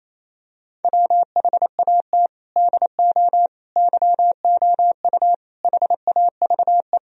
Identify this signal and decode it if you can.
Morse code, heard as a keyed tone, about 28 wpm, 700 Hz. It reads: WHAT DO YOU HAVE
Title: Sentences from Top 1000 Words - Review Part 2 - 28wpm